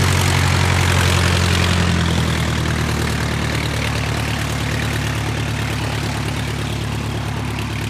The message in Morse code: none